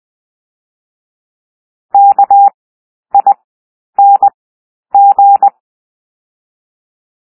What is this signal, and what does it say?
Morse code, heard as a keyed tone, about 20 wpm, 800 Hz. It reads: K I N G
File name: cwmp3 / W1 SOUND WORDS 20 wpm